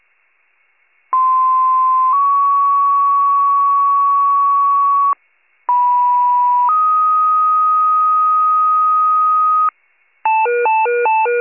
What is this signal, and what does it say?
Two-tone page, heard as a one-second tone, then a three-second tone, about 1040 then 1120 Hz, about 980 then 1250 Hz. It is heard from 1.1 to 9.7 s.
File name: pager / 95 RESCUE / wcalert